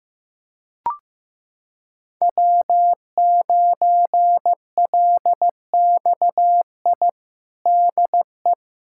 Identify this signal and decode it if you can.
Morse code, heard as a keyed tone, about 15 wpm, 700 Hz. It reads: W9LXI DE
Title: USA Callsign DE Callsign - 5 Characters - 15wpm